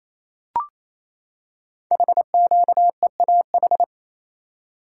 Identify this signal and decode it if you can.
Morse code, heard as a keyed tone, about 29 wpm, 700 Hz. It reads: HQEAH
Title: Code Groups - 5 Letters - 28wpm